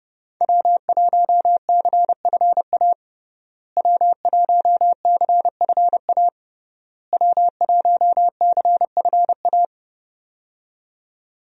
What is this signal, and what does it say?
Morse code, heard as a keyed tone, about 30 wpm, 700 Hz. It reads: W1CFA W1CFA W1CFA
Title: USA Call Signs - 5 Characters - Repeated 3x - 30wpm